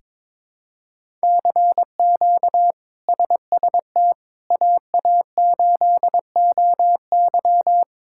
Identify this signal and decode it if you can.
Morse code, heard as a keyed tone, about 21 wpm, 700 Hz. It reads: CQ SST AA8OY
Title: K1USN SST QSO Practice - 22wpm